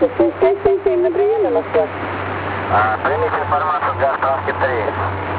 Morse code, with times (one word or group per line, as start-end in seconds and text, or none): none